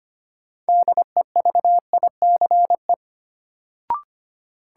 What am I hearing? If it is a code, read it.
Morse code, heard as a keyed tone, about 25 wpm, 700 Hz. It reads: DEVICE
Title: Sentences from Top 1500 Words - New Word Drill - 25wpm